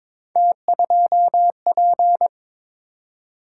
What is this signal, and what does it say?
Morse code, heard as a keyed tone, about 22 wpm, 700 Hz. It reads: T2P
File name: practice / 22wpm / Three Letter-Number Combinations - 22wpm